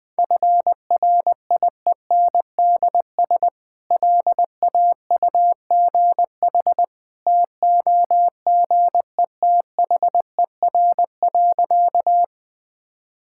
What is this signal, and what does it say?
Morse code, heard as a keyed tone, about 21 wpm, 700 Hz. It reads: FRIENDS LAUGH TOGETHER.
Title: Easy Sentences with Hints - 20wpm